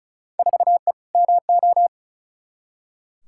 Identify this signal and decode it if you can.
Morse code, heard as a keyed tone, about 34 wpm, 700 Hz. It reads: 4E MO
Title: Field Day Exchanges Encore - 35wpm